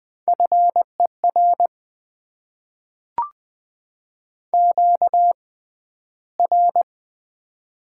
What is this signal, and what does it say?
Morse code, heard as a keyed tone, about 20 wpm, spaced as for 12 wpm, 700 Hz. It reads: FER QR
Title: Top 20 QSO Elements - 6x Character Spacing - 20wpm